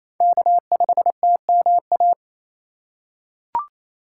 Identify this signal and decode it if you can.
Morse code, heard as a keyed tone, about 28 wpm, 700 Hz. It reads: K5TMA
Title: USA Call Signs - 5 Characters - 4x Character Spacing - 28wpm